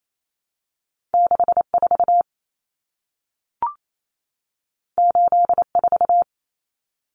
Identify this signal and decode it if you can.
Morse code, heard as a keyed tone, about 28 wpm, 700 Hz. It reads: 64 84
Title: Numbers - 2-Digit 28wpm